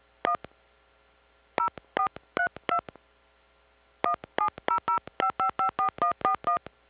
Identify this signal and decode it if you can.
DTMF, heard as telephone touch tones, tones of about 100 ms. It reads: 1 0432 17005554141